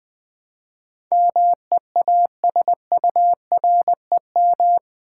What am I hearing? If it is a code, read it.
Morse code, heard as a keyed tone, about 20 wpm, 700 Hz. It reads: MEASUREM
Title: Build A Word - 4 Letters At A Time - 11-Letter Words - 20wpm